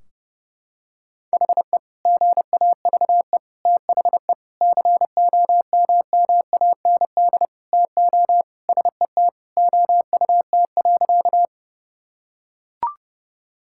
Morse code, heard as a keyed tone, about 31 wpm, 700 Hz. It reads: HE GAVE THE COMMAND TO SET OUT.
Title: Sentences from Top 1600 Words 30wpm